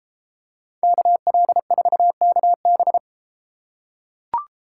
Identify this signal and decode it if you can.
Morse code, heard as a keyed tone, about 33 wpm, 700 Hz. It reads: KL4KB